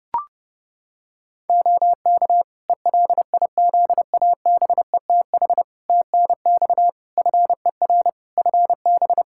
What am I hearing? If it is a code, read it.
Morse code, heard as a keyed tone, about 32 wpm, 700 Hz. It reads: OK ELIZABETH TNX FER FB
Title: Practice QSO - Exchange 4 - Ending - 30wpm